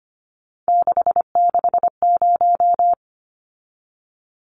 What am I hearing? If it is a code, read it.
Morse code, heard as a keyed tone, about 25 wpm, 690 Hz. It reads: 660
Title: Numbers - 3-Digit 25wpm